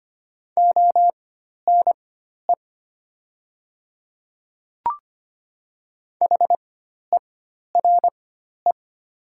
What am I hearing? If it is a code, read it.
Morse code, heard as a keyed tone, about 25 wpm, spaced as for 12 wpm, 690 Hz. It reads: ONE HERE
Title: 4x Character Spacing - Top 200 Words - 25wpm